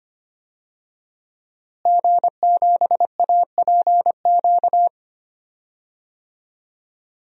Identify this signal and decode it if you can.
Morse code, heard as a keyed tone, about 25 wpm, 700 Hz. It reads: G7APQ
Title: DXCC-100 Call Signs - 5 Characters - Repeated 3x - 25wpm